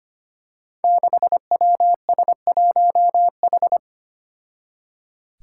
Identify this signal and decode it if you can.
Morse code, heard as a keyed tone, about 25 wpm, 700 Hz. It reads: 6WS1H
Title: Five Challenging Letter-Number Combinations 25wpm